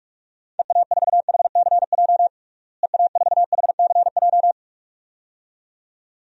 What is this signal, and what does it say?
Morse code, heard as a keyed tone, about 46 wpm, 700 Hz. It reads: EA4HCJ EA4HCJ
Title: DXCC-100 Call Signs - 6 Characters - Repeated 2x - 45wpm